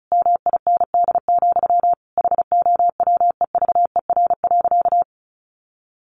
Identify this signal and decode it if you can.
Morse code, heard as a keyed tone, about 34 wpm, 700 Hz. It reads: MIND, HOWEVER.
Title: Sentences from Top 1100 Words 35wpm